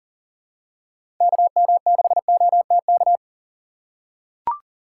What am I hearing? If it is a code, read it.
Morse code, heard as a keyed tone, about 40 wpm, 700 Hz. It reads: KM6OTK
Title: USA Call Signs - 6 Characters - 4x Character Spacing - 40wpm